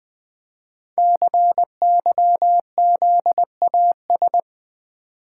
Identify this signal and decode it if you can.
Morse code, heard as a keyed tone, about 20 wpm, 700 Hz. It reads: CYZAS
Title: Code Groups - 5 Letters - 20wpm